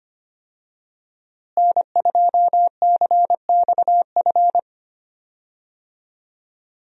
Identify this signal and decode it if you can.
Morse code, heard as a keyed tone, about 25 wpm, 700 Hz. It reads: N2CXF
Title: USA Call Signs - 5 Characters - Repeated 3x - 25wpm